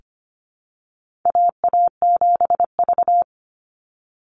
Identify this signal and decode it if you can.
Morse code, heard as a keyed tone, about 25 wpm, 700 Hz. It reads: AA7V
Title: Call Signs Repeated 2x - 25wpm